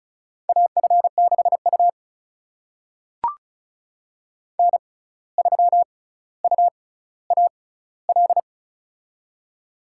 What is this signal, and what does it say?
Morse code, heard as a keyed tone, about 36 wpm, spaced as for 17 wpm, 700 Hz. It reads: AF6U N3UAL
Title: USA Call Signs - 6x Character Spacing - 35wpm